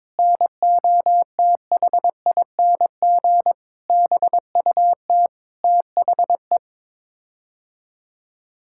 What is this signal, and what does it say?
Morse code, heard as a keyed tone, about 22 wpm, 700 Hz. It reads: NOTHING BUT THE
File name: Sets of 9 Words - 22wpm